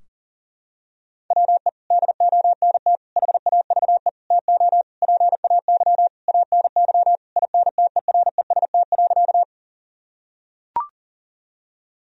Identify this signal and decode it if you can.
Morse code, heard as a keyed tone, about 40 wpm, 700 Hz. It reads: WE DONT HAVE TO PAY ANY INTEREST.
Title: Sentences from Top 1000 Words - Review Part 2 - 40wpm